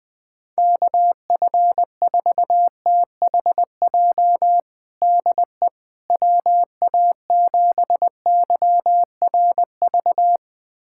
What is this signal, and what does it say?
Morse code, heard as a keyed tone, about 20 wpm, 700 Hz. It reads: KF4THJ DE WA7YRV
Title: USA Callsign DE Callsign 20wpm